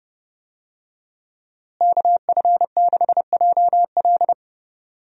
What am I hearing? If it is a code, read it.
Morse code, heard as a keyed tone, about 32 wpm, 700 Hz. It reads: KF6JL
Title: USA Call Signs - 5 Characters - Repeated 3x - 30wpm